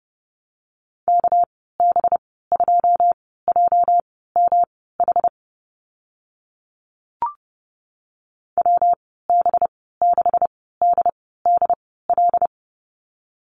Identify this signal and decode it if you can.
Morse code, heard as a keyed tone, about 30 wpm, spaced as for 17 wpm, 700 Hz. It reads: KB2JMH WB6DDL
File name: Lesson 37g - Letter x - Call Signs 30wpm